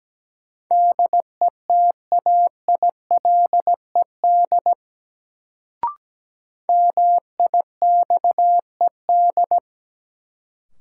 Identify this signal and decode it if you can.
Morse code, heard as a keyed tone, about 17 wpm, 700 Hz. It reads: DETAILED MIXED
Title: Sentences from Top 1500 Words - New Word Drill Alt - 17wpm